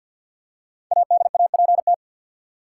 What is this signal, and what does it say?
Morse code, heard as a keyed tone, about 53 wpm, 700 Hz. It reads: ADAPT
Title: Five Letter Words - 50wpm